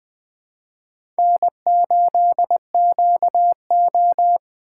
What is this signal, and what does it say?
Morse code, heard as a keyed tone, about 20 wpm, 700 Hz. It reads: N8QO
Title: USA Call Signs - 4x Character Spacing - 20wpm